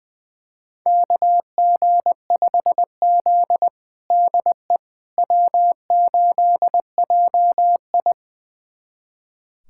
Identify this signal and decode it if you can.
Morse code, heard as a keyed tone, about 21 wpm, 700 Hz. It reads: KG5Z DE W8JI
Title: USA Callsign DE Callsign - 4 Characters - 20wpm